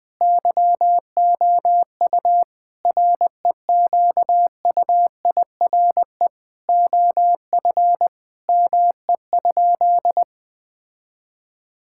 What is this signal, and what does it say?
Morse code, heard as a keyed tone, about 20 wpm, 700 Hz. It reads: YOU REQUIRE OF ME?
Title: Sentences from Top 1000 Words - Review Part 6 - 20wpm